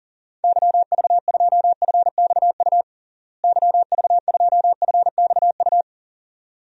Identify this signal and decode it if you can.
Morse code, heard as a keyed tone, about 40 wpm, 700 Hz. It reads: YV2FXU YV2FXU
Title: DXCC-100 Call Signs - 6 Characters - Repeated 2x - 40wpm